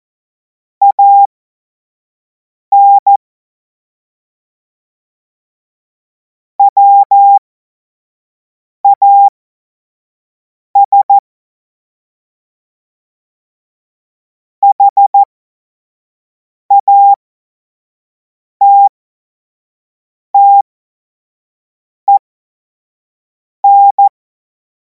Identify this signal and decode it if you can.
Morse code, heard as a keyed tone, about 14 wpm, spaced as for 5 wpm, 800 Hz. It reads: AN WAS HATTEN
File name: Morsekurs / Witze / Witz 03. März 2026 25 F70